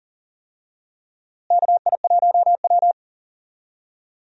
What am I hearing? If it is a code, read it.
Morse code, heard as a keyed tone, about 41 wpm, 700 Hz. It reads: KI1W
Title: USA Call Signs - 4 Characters - Repeated 3x - 40wpm